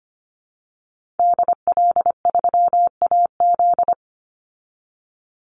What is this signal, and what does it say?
Morse code, heard as a keyed tone, about 25 wpm, 700 Hz. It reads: DL3AZ
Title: Call Signs Repeated 3x - 25wpm